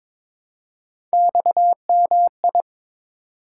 Morse code, heard as a keyed tone, about 22 wpm, 700 Hz. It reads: XMI
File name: practice / 22wpm / Three Letter-Number Combinations - 22wpm